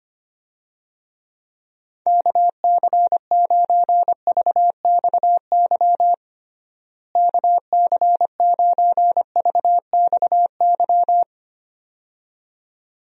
Morse code, heard as a keyed tone, about 25 wpm, 700 Hz. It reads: KC9VXY KC9VXY
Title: USA Call Signs - 6 Characters - Repeated 2x - 25wpm